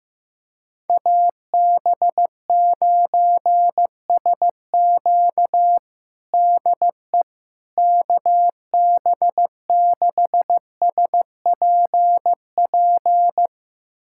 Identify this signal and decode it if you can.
Morse code, heard as a keyed tone, about 15 wpm, 700 Hz. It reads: AB9SQ DE KB6SPP
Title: USA Callsign DE Callsign 15wpm